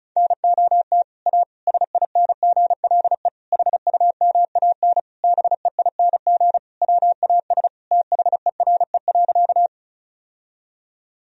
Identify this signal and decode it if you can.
Morse code, heard as a keyed tone, about 34 wpm, 700 Hz. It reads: NOT A SINGLE HUMAN BEING WAS THERE.
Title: Sentences from Top 1000 Words - Review Part 4 - 35wpm